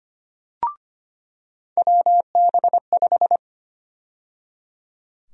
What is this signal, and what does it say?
Morse code, heard as a keyed tone, about 25 wpm, 700 Hz. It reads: WB5
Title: USA Call Sign Prefixes - 25wpm